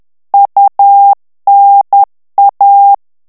Morse code, heard as a keyed tone, about 10 wpm, 800 Hz. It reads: UNA
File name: ITALIEN texte clair 2 vitesse 10 mots minute